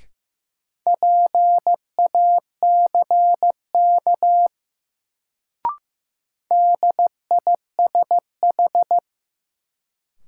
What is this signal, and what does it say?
Morse code, heard as a keyed tone, about 15 wpm, 700 Hz. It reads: PACK DISH
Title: Sentences from Top 1300 Words - New Word Drill - 15wpm